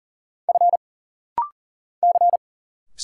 Morse code, heard as a keyed tone, about 40 wpm, 700 Hz. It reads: F C
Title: Single Letters - Rapid-Fire 40wpm